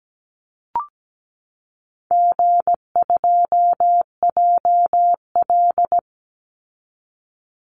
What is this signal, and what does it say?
Morse code, heard as a keyed tone, about 17 wpm, 700 Hz. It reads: G2JL
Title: Easy Call Signs Repeated 2x - 17wpm